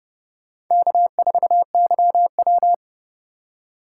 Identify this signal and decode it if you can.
Morse code, heard as a keyed tone, about 30 wpm, 700 Hz. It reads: K4YW
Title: USA Call Signs - 4 Characters - 4x Character Spacing - 30wpm